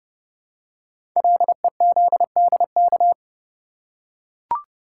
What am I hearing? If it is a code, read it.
Morse code, heard as a keyed tone, about 30 wpm, 690 Hz. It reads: LEZDK
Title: Code Groups - 5 Letters - 30wpm